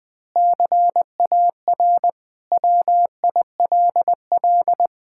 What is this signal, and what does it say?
Morse code, heard as a keyed tone, about 21 wpm, 700 Hz. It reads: CAR WILL
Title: Sentences from Top 1600 Words 20wpm